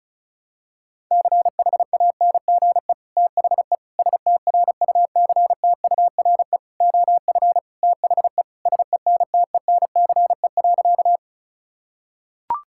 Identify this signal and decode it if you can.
Morse code, heard as a keyed tone, about 36 wpm, 700 Hz. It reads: CHANGE THE STRUCTURE OF THE SENTENCE.
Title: Sentences from Top 1000 Words - Review Part 6 - 35wpm